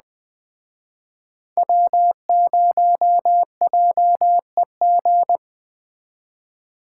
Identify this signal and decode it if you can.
Morse code, heard as a keyed tone, about 20 wpm, 700 Hz. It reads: W0JEG